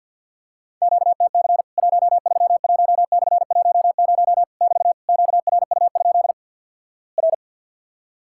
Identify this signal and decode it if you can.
Morse code, heard as a keyed tone, about 50 wpm, 700 Hz. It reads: QTC 131/10 <BT> QRU? R
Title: Worked All Europe DX Contest - QTC Exchange - 50wpm